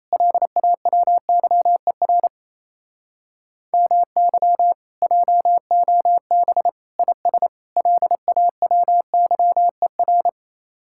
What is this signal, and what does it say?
Morse code, heard as a keyed tone, about 36 wpm, 700 Hz. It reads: LAWYER MY JOB IS LAWYER